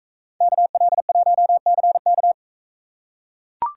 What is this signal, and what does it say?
Morse code, heard as a keyed tone, about 39 wpm, 700 Hz. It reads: KL1CK